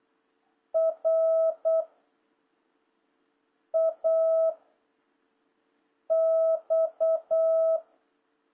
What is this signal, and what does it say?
Morse code, heard as a keyed tone, about 8 wpm, 660 Hz. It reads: R A X
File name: jh8cbh / 2026cw